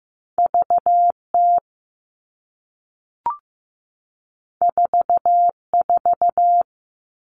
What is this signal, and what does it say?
Morse code, heard as a keyed tone, about 15 wpm, 700 Hz. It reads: VT 44
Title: All Cut Numbers - 2-Digit 15wpm